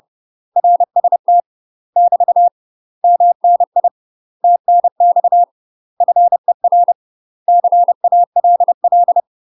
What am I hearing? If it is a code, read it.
Morse code, heard as a keyed tone, about 30 wpm, 700 Hz. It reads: RST <BT> MNI TNX FER CALL
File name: UK MASTER QSOs TRAINING / 30WPM / UK MASTER QSOs 0043